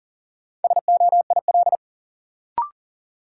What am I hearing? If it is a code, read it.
Morse code, heard as a keyed tone, about 40 wpm, 700 Hz. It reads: SOIL
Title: Sentences from Top 1500 Words - New Word Drill Alt - 40wpm